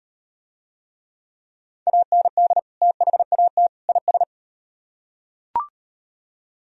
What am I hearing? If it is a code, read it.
Morse code, heard as a keyed tone, about 37 wpm, 700 Hz. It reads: AND THAT IS